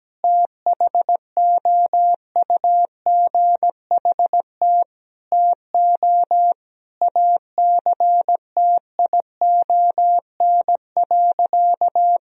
Morse code, heard as a keyed tone, about 17 wpm, 700 Hz. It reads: THOUGHT TO ACTION.
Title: Sentences from Top 1000 Words - Review Part 4 - 17wpm